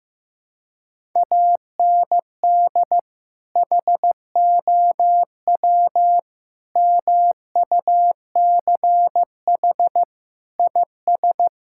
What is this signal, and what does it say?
Morse code, heard as a keyed tone, about 15 wpm, 700 Hz. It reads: AND HOW MUCH IS
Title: Sentences from Top 1000 Words - Review Part 5 - 15wpm